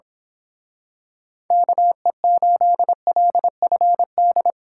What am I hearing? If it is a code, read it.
Morse code, heard as a keyed tone, about 25 wpm, 700 Hz. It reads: KE8LFD
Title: Speed Racing - USA Call Signs 22wpm